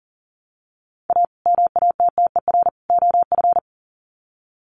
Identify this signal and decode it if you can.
Morse code, heard as a keyed tone, about 40 wpm, 700 Hz. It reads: A MATTER OF
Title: Sets of 3 Words 40wpm